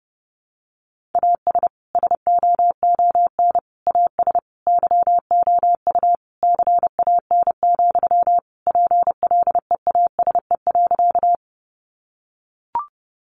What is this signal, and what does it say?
Morse code, heard as a keyed tone, about 30 wpm, 700 Hz. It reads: AS SOON AS YOU CAN, PLEASE.
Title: Sentences from Top 1500 Words 30wpm